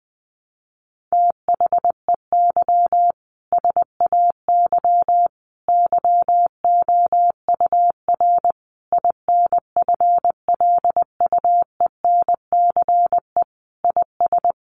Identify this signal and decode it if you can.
Morse code, heard as a keyed tone, about 19 wpm, 700 Hz. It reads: THEY SAY YOUR INFLUENCE IS
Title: Sentences from Top 1100 Words 20wpm